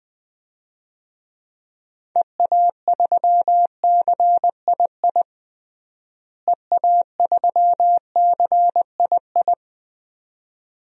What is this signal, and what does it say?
Morse code, heard as a keyed tone, about 20 wpm, 700 Hz. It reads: EA3CII EA3CII
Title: DXCC-100 Call Signs - 6 Characters - Repeated 2x - 20wpm